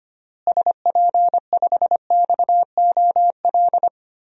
Morse code, heard as a keyed tone, about 25 wpm, 700 Hz. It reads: SP5XOL
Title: DXCC-100 Call Signs - 6 Characters - Repeated 3x - 25wpm